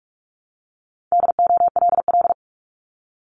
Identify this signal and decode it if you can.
Morse code, heard as a keyed tone, about 46 wpm, 700 Hz. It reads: DOLL
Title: Multi-letter Phonemes Repeated 3x 45wpm